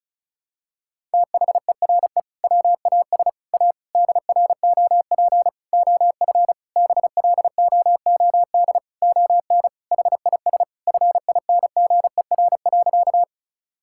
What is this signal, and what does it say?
Morse code, heard as a keyed tone, about 36 wpm, 700 Hz. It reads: THERE WAS A DROP OF BLOOD ON HIS FINGER.
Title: Sentences from Top 1600 Words 35wpm